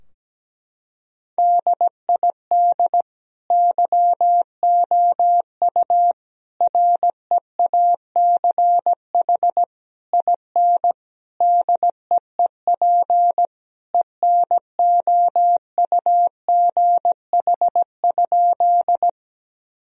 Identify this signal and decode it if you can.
Morse code, heard as a keyed tone, about 17 wpm, 700 Hz. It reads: DID YOU REACH IN DEEP ENOUGH?
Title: Sentences from Top 1000 Words - Review Part 3 - 17wpm